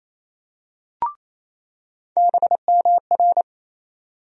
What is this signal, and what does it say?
Morse code, heard as a keyed tone, about 28 wpm, 700 Hz. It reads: BMR
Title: USA Call Sign Postfixes - 3 Letters 28wpm